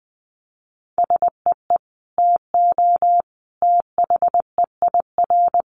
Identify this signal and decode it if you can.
Morse code, heard as a keyed tone, about 20 wpm, 700 Hz. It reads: SEE TO THEIR
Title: Sentences from Top 1100 Words 20wpm